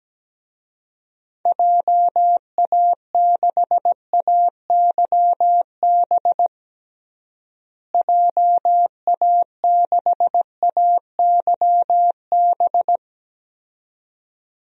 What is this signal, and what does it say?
Morse code, heard as a keyed tone, about 17 wpm, 700 Hz. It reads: JA6AYB JA6AYB
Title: DXCC-100 Call Signs - 6 Characters - Repeated 2x - 17wpm